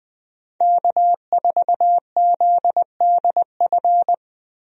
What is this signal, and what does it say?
Morse code, heard as a keyed tone, about 21 wpm, 700 Hz. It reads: K4ZDF
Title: USA Call Signs - 5 Characters - Repeated 3x - 20wpm